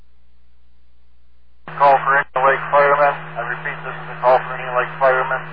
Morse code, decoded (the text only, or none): none